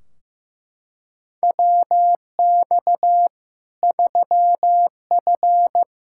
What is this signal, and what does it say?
Morse code, heard as a keyed tone, about 16 wpm, 700 Hz. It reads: WX 3F
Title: Practice QSO - Exchange 2 - Info - 15wpm